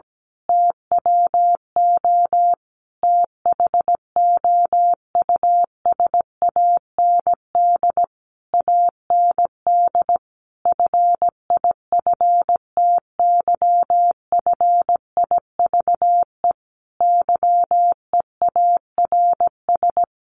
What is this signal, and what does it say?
Morse code, heard as a keyed tone, about 17 wpm, 700 Hz. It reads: TWO THOUSAND AND FIFTYFIVE YEARS
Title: A Sound Of Thunder 17wpm